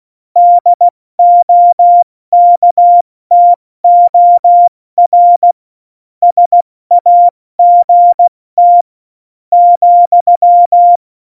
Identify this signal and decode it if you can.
Morse code, heard as a keyed tone, about 16 wpm, 700 Hz. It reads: DOKTOR SAGT ,
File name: Morsekurs / Witze / Witz 09. Januar 2018, 70 F80